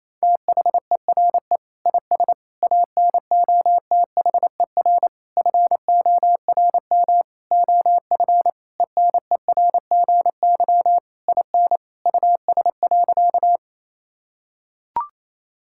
Morse code, heard as a keyed tone, about 27 wpm, 700 Hz. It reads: THERE IS ANOTHER FORM OF ENERGY IN US.
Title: Sentences from Top 1000 Words - Review Part 6 - 28wpm